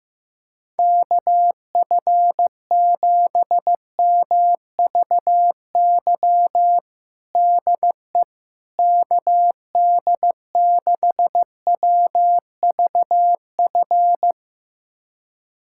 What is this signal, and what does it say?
Morse code, heard as a keyed tone, about 15 wpm, 700 Hz. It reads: KF7MVY DE KD6WVF
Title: USA Callsign DE Callsign - 6 Characters - 15wpm